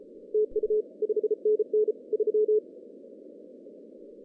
Morse code, heard as a keyed tone, about 34 wpm, 430 Hz. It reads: TU 5NN 3
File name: 2015 cq ww cw / 2015 cqwww k5tr - 0-18